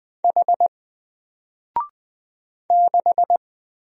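Morse code, heard as a keyed tone, about 20 wpm, 700 Hz. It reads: H 6
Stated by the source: Single letter and number Morse drill at 20 wpm, from N4LCD.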